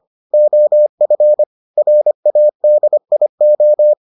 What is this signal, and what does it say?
Morse code, heard as a keyed tone, about 25 wpm, 600 Hz. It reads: OF RADIO